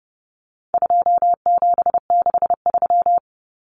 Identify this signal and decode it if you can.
Morse code, heard as a keyed tone, about 30 wpm, 700 Hz. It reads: 2763
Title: Common Cut Numbers - 4-Digit 30wpm